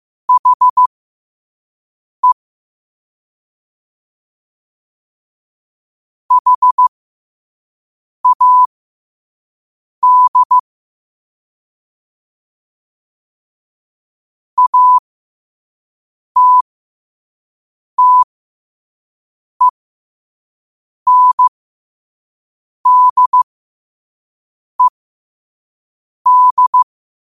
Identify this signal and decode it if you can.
Morse code, heard as a keyed tone, about 15 wpm, spaced as for 5 wpm, 1000 Hz. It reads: HE HAD ATTENDED
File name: qotdfiles / 5wpm